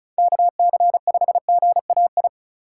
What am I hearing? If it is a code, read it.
Morse code, heard as a keyed tone, about 45 wpm, 700 Hz. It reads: KC5GAI